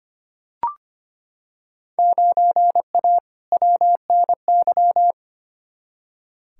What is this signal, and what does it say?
Morse code, heard as a keyed tone, about 25 wpm, 700 Hz. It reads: 9A WNY
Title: Field Day Exchanges Encore - 25wpm